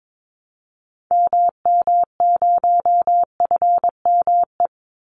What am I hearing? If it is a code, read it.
Morse code, heard as a keyed tone, about 22 wpm, 700 Hz. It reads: MM0FME
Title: Call Signs - 22wpm